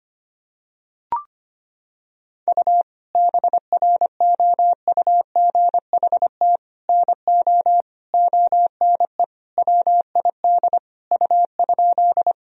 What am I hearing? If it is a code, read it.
Morse code, heard as a keyed tone, about 25 wpm, 700 Hz. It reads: U BROUGHT NO ONE WID U?
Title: Sentences from Top 500 Words and Abbreviations 25wpm